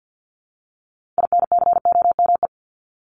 Morse code, heard as a keyed tone, about 50 wpm, 700 Hz. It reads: INCOME
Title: Top 751-1000 Words 50wpm